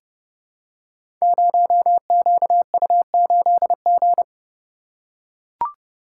Morse code, heard as a keyed tone, about 30 wpm, 700 Hz. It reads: 0QU8G
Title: Code Groups - 5 Letter:Numbers - 30wpm